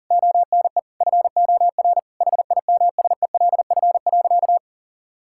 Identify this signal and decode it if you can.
Morse code, heard as a keyed tone, about 37 wpm, 700 Hz. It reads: ONE FOR HIMSELF.
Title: Sentences from Top 1600 Words 40wpm